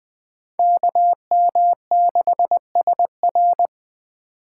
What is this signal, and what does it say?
Morse code, heard as a keyed tone, about 20 wpm, 700 Hz. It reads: KM6SR
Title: USA Call Signs - 5 Characters - 3x Character Spacing - 20wpm